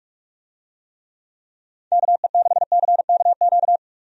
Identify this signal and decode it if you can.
Morse code, heard as a keyed tone, about 46 wpm, 700 Hz. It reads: KE6CKQ
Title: USA Call Signs - 6 Characters - 45wpm